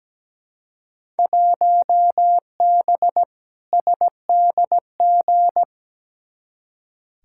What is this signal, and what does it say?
Morse code, heard as a keyed tone, about 17 wpm, 700 Hz. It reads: 1B SDG
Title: Field Day Exchanges Encore - 17wpm